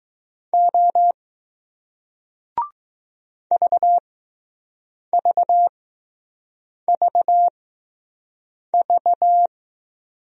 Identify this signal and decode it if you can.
Morse code, heard as a keyed tone, about 20 wpm, spaced as for 6 wpm, 700 Hz. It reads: O VVVV